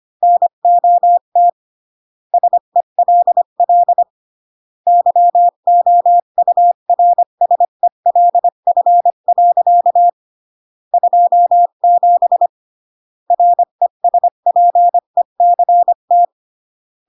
Morse code, heard as a keyed tone, about 25 wpm, 700 Hz. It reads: NOT SELL YOURSELF. 27 RESPECT